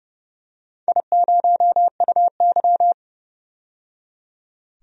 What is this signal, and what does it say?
Morse code, heard as a keyed tone, about 31 wpm, 700 Hz. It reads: I0UY
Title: DXCC-100 Call Signs - 4 Characters - 30wpm